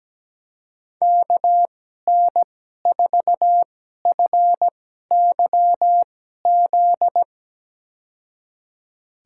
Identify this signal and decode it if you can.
Morse code, heard as a keyed tone, about 17 wpm, spaced as for 12 wpm, 700 Hz. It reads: KN4FYZ
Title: USA Call Signs - 2x Character Spacing - 17wpm